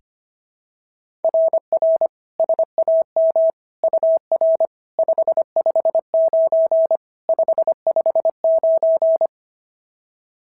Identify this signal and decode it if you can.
Morse code, heard as a keyed tone, about 25 wpm, 690 Hz. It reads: RR SAM UR 559 559
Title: SOTA QSO Practice 25wpm